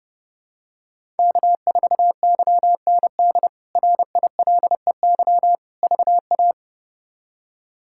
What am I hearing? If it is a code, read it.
Morse code, heard as a keyed tone, about 31 wpm, 700 Hz. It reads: K4YND RILEY VA
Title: K1USN SST QSO Practice - 30wpm